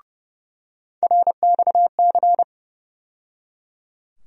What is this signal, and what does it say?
Morse code, heard as a keyed tone, about 30 wpm, 700 Hz. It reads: RXC
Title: USA Call Sign Postfixes - 3 Letters 30wpm